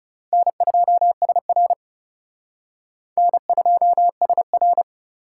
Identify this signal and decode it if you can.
Morse code, heard as a keyed tone, about 36 wpm, 700 Hz. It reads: N2SR N2SR
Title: Speed Racing - USA 1x2 Call Signs 30wpm